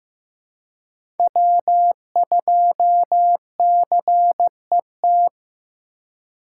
Morse code, heard as a keyed tone, about 15 wpm, 700 Hz. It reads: W2CET
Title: USA Call Signs - 5 Characters - 4x Character Spacing - 15wpm